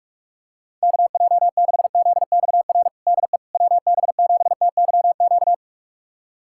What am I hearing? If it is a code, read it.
Morse code, heard as a keyed tone, about 46 wpm, 700 Hz. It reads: KJ6ZXR DE WB7TYQ
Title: USA Callsign DE Callsign - 6 Characters - 45wpm